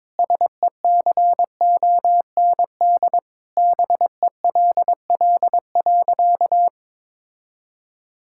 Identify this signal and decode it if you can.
Morse code, heard as a keyed tone, about 22 wpm, 700 Hz. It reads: SECOND BELL.
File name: practice / 22wpm / Sentences with 2-Word Phrases - 22wpm